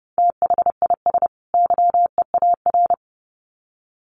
Morse code, heard as a keyed tone, about 30 wpm, 700 Hz. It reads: THIS YEAR
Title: Sets of 2 Words 30wpm